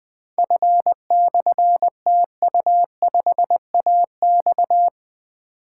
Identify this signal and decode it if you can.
Morse code, heard as a keyed tone, about 20 wpm, 700 Hz. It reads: F/TU5AX
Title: Difficult Call Signs 20wpm